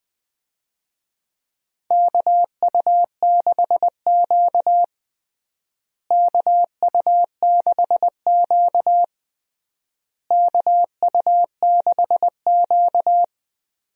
Morse code, heard as a keyed tone, about 21 wpm, 700 Hz. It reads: KU6Q KU6Q KU6Q
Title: USA Call Signs - 4 Characters - Repeated 3x - 20wpm